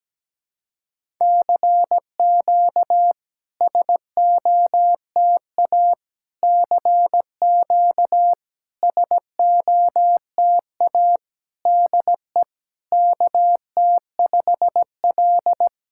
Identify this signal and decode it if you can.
Morse code, heard as a keyed tone, about 17 wpm, 690 Hz. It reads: CQ SOTA CQ SOTA DE KT5L
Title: SOTA QSO Practice 17wpm